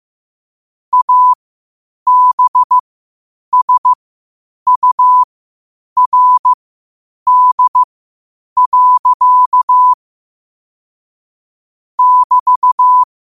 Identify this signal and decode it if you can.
Morse code, heard as a keyed tone, about 15 wpm, spaced as for 8 wpm, 1000 Hz. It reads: ABSURD. <BT>
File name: qotdfiles / 8wpm